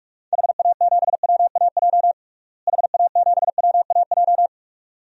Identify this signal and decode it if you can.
Morse code, heard as a keyed tone, about 47 wpm, 700 Hz. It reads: HA7WAJ HA7WAJ
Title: DXCC-100 Call Signs - 6 Characters - Repeated 2x - 45wpm